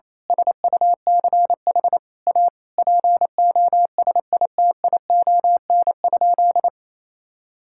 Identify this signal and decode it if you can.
Morse code, heard as a keyed tone, about 27 wpm, 700 Hz. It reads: SUCH A POSITION?
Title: Sentences from Top 1000 Words - Review Part 5 - 28wpm